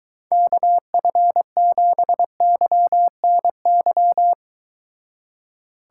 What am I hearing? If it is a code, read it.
Morse code, heard as a keyed tone, about 23 wpm, 700 Hz. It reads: KF7YNY